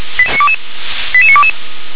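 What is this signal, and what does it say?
Five-tone selective call (ZVEI), heard as quick runs of five tones, tones of about 70 ms. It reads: 88922
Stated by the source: ZVEI five-tone selective call 88081 received by SDR.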